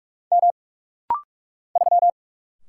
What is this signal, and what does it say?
Morse code, heard as a keyed tone, about 46 wpm, 700 Hz. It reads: M 3
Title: Single Letter-Number - Rapid-Fire 45wpm